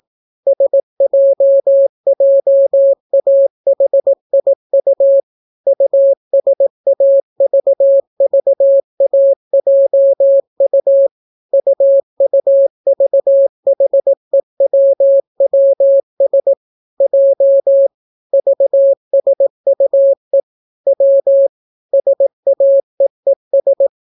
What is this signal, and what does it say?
Morse code, heard as a keyed tone, about 18 wpm, 550 Hz. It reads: SJJAHIU USAVVAJU UUVHEWWS J VSUE W SAEES